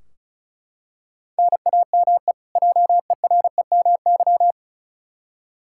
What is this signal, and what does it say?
Morse code, heard as a keyed tone, about 36 wpm, 700 Hz. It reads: NAME JEREMY
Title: K1USN SST QSO Practice 35wpm